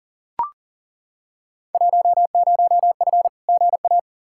Morse code, heard as a keyed tone, about 37 wpm, 700 Hz. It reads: 10F GA
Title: Field Day Exchanges Encore - 40wpm